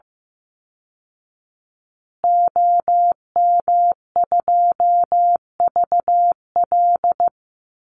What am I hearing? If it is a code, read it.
Morse code, heard as a keyed tone, about 15 wpm, 700 Hz. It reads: OM2VL
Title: Call Signs Repeated 2x - 15wpm